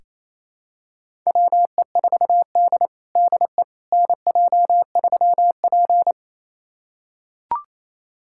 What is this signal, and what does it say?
Morse code, heard as a keyed tone, about 28 wpm, 700 Hz. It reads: WE4D DE NJ3P
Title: USA Callsign DE Callsign - 4 Characters - 28wpm